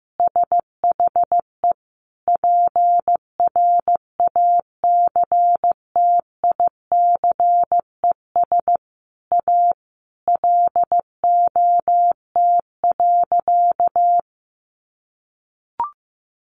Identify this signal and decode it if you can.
Morse code, heard as a keyed tone, about 15 wpm, 700 Hz. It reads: SHE PRACTICES A LOT.
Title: Additional Easy Phrases 15wpm